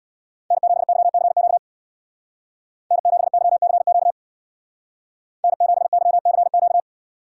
Morse code, heard as a keyed tone, about 60 wpm, 700 Hz. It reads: N7YCQ N7YCQ N7YCQ
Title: Speed Racing - USA 1x3 Call Signs 50wpm